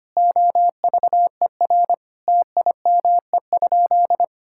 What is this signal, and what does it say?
Morse code, heard as a keyed tone, about 25 wpm, 700 Hz. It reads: OVER TIME?
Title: Sentences from Top 1000 Words - Review Part 5 - 25wpm